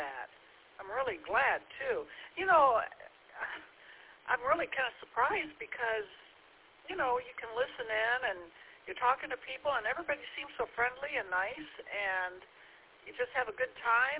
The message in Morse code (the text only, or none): none